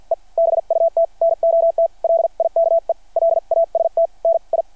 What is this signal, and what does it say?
Morse code, heard as a keyed tone, about 53 wpm, spaced as for 43 wpm, 660 Hz. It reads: E BUT NOT LIKE LAST NI
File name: archived / CRSnet